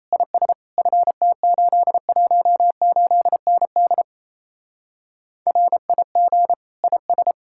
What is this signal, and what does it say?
Morse code, heard as a keyed tone, about 35 wpm, 700 Hz. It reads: IS FT818ND RIG IS